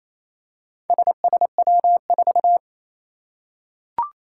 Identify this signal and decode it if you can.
Morse code, heard as a keyed tone, about 27 wpm, 700 Hz. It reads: SSW4
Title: Four Challenging Letter-Number Combinations 28wpm